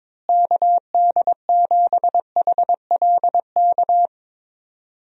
Morse code, heard as a keyed tone, about 21 wpm, 700 Hz. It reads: KD7HLK
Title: USA Call Signs - 6 Characters - Repeated 2x - 22wpm